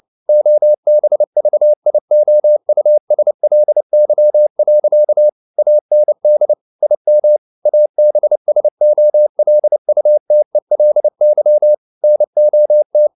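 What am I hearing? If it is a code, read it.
Morse code, heard as a keyed tone, about 29 wpm, 600 Hz. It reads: OBVIOUSLY. AND IM ABSOLUTELY NOT
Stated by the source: Morse code practice at 28 wpm, text of Ken Follett's novel Never in English, from CW med Gnister.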